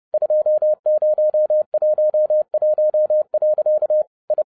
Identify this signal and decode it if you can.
Morse code, heard as a keyed tone, about 30 wpm, 610 Hz. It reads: 2011. I
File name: 30wpm